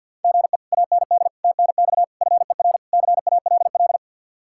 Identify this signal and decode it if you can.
Morse code, heard as a keyed tone, about 64 wpm, 700 Hz. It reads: GE AND TNX FER CALL